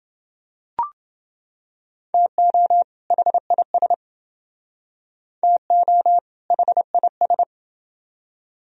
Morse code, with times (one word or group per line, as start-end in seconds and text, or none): 2.1-2.8 TO
3.1-3.9 HIS
5.4-6.2 TO
6.5-7.4 HIS